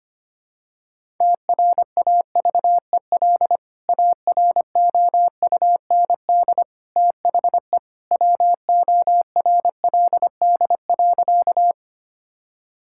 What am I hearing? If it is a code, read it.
Morse code, heard as a keyed tone, about 25 wpm, 700 Hz. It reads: TRAVEL AROUND THE WORLD.
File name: practice / 25wpm / Sentences with 3-Word Phrases - 25wpm